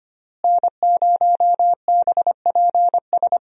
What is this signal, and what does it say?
Morse code, heard as a keyed tone, about 25 wpm, 700 Hz. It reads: N0BPS
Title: USA Call Signs - 5 Characters - Repeated 2x - 25wpm